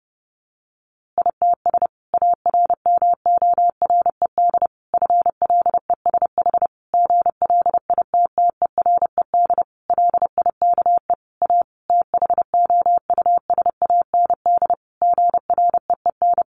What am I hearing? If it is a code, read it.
Morse code, heard as a keyed tone, about 30 wpm, 700 Hz. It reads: ITS ARMORED FLESH GLITTERED LIKE A THOUSAND GREEN
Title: A Sound Of Thunder 30wpm